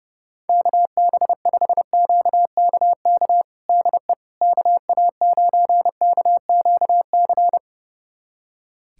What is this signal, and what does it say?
Morse code, heard as a keyed tone, about 30 wpm, 700 Hz. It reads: KB5QKK DE KA9KQC
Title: USA Callsign DE Callsign - 6 Characters - 30wpm